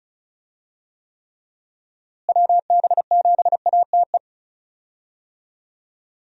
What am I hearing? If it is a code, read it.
Morse code, heard as a keyed tone, about 36 wpm, 700 Hz. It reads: WB7ATE
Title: USA Call Signs - 6 Characters - Repeated 2x - 35wpm